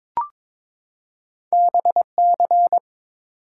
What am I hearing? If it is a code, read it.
Morse code, heard as a keyed tone, about 22 wpm, 700 Hz. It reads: BC
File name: ARRL Sections v2 - 22wpm